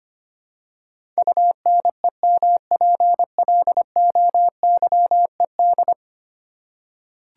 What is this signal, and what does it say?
Morse code, heard as a keyed tone, about 25 wpm, 700 Hz. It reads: UNEMPLOYED
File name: 10 Letter Words - First 3 Letters Given - 25wpm